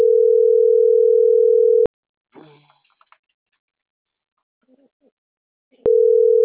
Ringback tone (North America), 440 + 480 Hz.